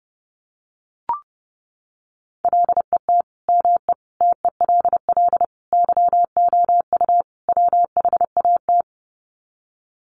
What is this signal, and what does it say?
Morse code, heard as a keyed tone, about 30 wpm, 700 Hz. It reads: LET ME TELL YOU WHAT
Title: Sets of 5 Words Encore 30wpm